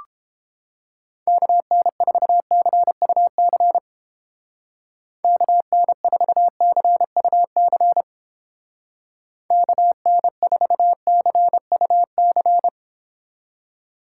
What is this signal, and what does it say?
Morse code, heard as a keyed tone, about 31 wpm, 700 Hz. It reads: KN4CUC KN4CUC KN4CUC